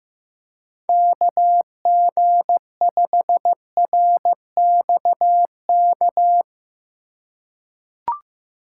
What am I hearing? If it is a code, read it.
Morse code, heard as a keyed tone, about 16 wpm, 700 Hz. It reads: KG5RXK
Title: USA Call Signs - 6 Characters - 4x Character Spacing - 15wpm